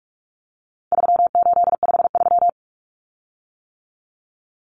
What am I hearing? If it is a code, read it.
Morse code, heard as a keyed tone, about 44 wpm, 700 Hz. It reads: 3853
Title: Common Cut Numbers - 4-Digit 45wpm